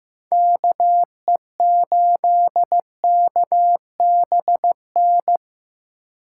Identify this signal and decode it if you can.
Morse code, heard as a keyed tone, about 15 wpm, 700 Hz. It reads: KE8KBN
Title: USA Call Signs - 6 Characters - 4x Character Spacing - 15wpm